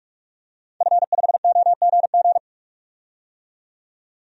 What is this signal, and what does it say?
Morse code, heard as a keyed tone, about 46 wpm, 700 Hz. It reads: F5OGG
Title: DXCC-100 Call Signs - 5 Characters - 45wpm